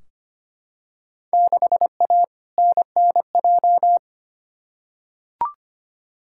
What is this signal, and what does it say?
Morse code, heard as a keyed tone, about 25 wpm, 700 Hz. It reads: 6A NNJ
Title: Field Day Exchanges Encore - 25wpm